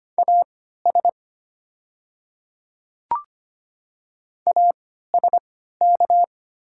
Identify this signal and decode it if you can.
Morse code, heard as a keyed tone, about 25 wpm, spaced as for 14 wpm, 700 Hz. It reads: AS ASK
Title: 3x Character Spacing - Top 300 Words - 25wpm